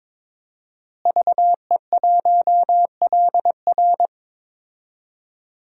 Morse code, heard as a keyed tone, about 21 wpm, 700 Hz. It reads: VE1LR
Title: DXCC-100 Call Signs - 5 Characters - Repeated 2x - 22wpm